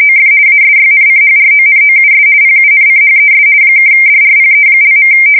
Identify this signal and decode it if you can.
RTTY, about 50 baud. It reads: OF ASYNCHRONOUS PUNCHES (DRPE).
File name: fsk170-50B / teletype-corp